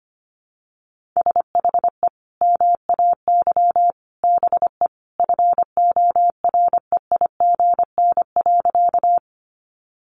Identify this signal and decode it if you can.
Morse code, heard as a keyed tone, about 25 wpm, 700 Hz. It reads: SHE MAY BE FOREIGN.
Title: Sentences from Top 1400 Words 25wpm